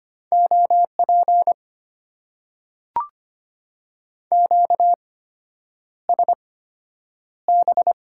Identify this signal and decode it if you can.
Morse code, heard as a keyed tone, about 25 wpm, spaced as for 9 wpm, 690 Hz. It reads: OP QSB
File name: Top 90 QSO Elements - 8x Character Spacing - 25wpm